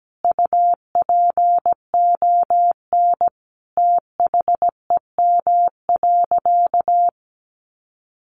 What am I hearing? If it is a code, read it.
Morse code, heard as a keyed tone, about 17 wpm, 700 Hz. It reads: UPON THEM.
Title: Sentences from Top 1200 Words 17wpm